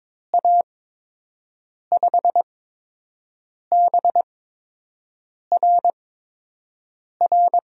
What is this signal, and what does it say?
Morse code, heard as a keyed tone, about 22 wpm, spaced as for 6 wpm, 700 Hz. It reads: A5BRR
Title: USA Call Signs - 6 Characters - 8x Character Spacing - 22wpm